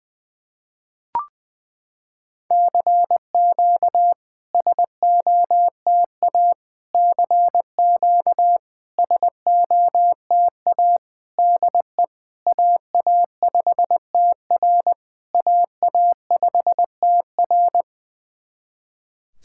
Morse code, heard as a keyed tone, about 20 wpm, 690 Hz. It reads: CQ SOTA CQ SOTA DE AA5TR AA5TR
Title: SOTA QSO Practice 20wpm